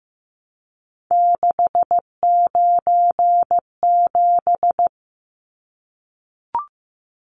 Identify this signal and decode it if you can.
Morse code, heard as a keyed tone, about 15 wpm, 700 Hz. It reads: 697
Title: Numbers - 3-Digit 15wpm